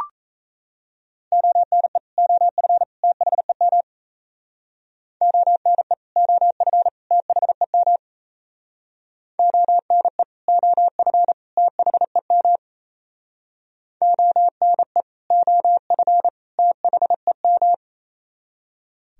Morse code, heard as a keyed tone, about 34 wpm, 700 Hz. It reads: ONE OF THEM ONE OF THEM ONE OF THEM ONE OF THEM
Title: Speed-Racing - Sets of 3 Words 28wpm